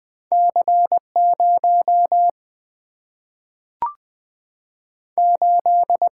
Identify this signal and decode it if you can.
Morse code, heard as a keyed tone, about 20 wpm, 700 Hz. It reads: C 0 8
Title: Single-Letter Single-Number 20wpm